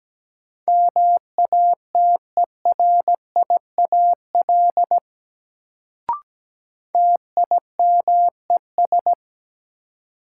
Morse code, heard as a keyed tone, about 17 wpm, 700 Hz. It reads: MATERIAL TIMES
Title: Sentences from Top 1100 Words - New Word Drill Alt - 17wpm